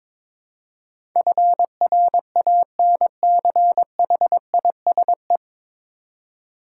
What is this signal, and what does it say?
Morse code, heard as a keyed tone, about 22 wpm, 700 Hz. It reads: FRANCHISE
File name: 9 Letter Words - First 3 Letters Given - 22wpm